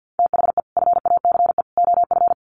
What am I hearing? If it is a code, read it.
Morse code, heard as a keyed tone, about 50 wpm, 700 Hz. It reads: THE FACE OF
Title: Sets of 3 Words 50wpm